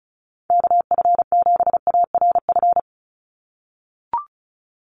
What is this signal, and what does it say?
Morse code, heard as a keyed tone, about 34 wpm, 700 Hz. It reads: KF7ARF
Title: USA Call Signs 35wpm